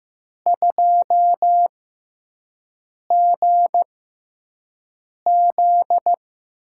Morse code, heard as a keyed tone, about 15 wpm, spaced as for 5 wpm, 700 Hz. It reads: 2GZ
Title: USA Call Signs - 4 Characters - 6x Character Spacing - 15wpm